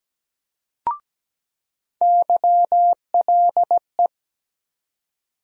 Morse code, heard as a keyed tone, about 17 wpm, 700 Hz. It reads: YLE
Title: USA Call Sign Postfixes - 3 Letters 17wpm